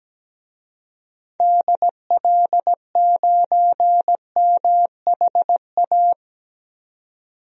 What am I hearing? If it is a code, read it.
Morse code, heard as a keyed tone, about 17 wpm, 700 Hz. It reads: DL9MHA
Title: DXCC-100 Call Signs - 6 Characters - 17wpm